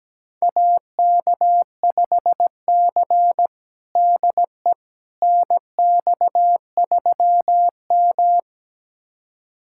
Morse code, heard as a keyed tone, about 17 wpm, 700 Hz. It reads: AK5C DE NX3M
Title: USA Callsign DE Callsign - 4 Characters - 17wpm